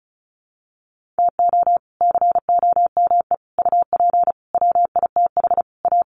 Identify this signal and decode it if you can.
Morse code, heard as a keyed tone, about 34 wpm, 700 Hz. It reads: TO COME UP WITH A
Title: Sets of 5 Words 35wpm